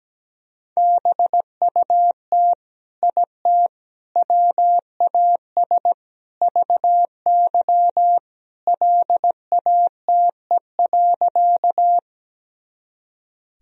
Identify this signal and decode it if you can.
Morse code, heard as a keyed tone, about 17 wpm, 700 Hz. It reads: BUT IT WAS VY LATE.
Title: Sentences from Top 300 Words and Abbreviations 17wpm